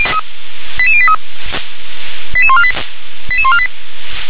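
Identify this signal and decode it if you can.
Five-tone selective call (ZVEI), heard as quick runs of five tones, tones of about 70 ms. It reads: 88083 88148 88148
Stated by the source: Five-tone ZVEI selective call received on an SDR.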